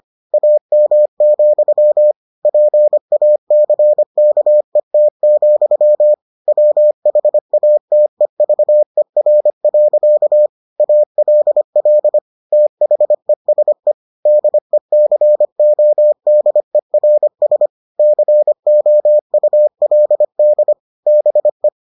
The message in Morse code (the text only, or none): AM, PACKET, WHATEVER. ALL THESE DECODERS COULD BE